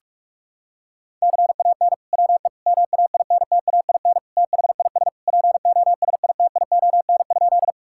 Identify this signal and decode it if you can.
Morse code, heard as a keyed tone, about 41 wpm, 700 Hz. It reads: CAN WE MAINTAIN THIS POSITION?
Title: Sentences from Top 1000 Words - Review Part 5 - 45wpm